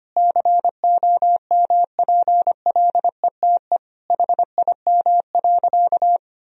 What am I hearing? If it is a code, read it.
Morse code, heard as a keyed tone, about 25 wpm, 700 Hz. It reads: COMPLETE HIM.
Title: Sentences from Top 1000 Words - Review Part 2 - 25wpm